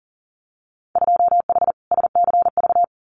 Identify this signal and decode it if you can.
Morse code, heard as a keyed tone, about 40 wpm, 700 Hz. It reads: 2H SCV
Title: Winter Field Day Exchanges Encore 40wpm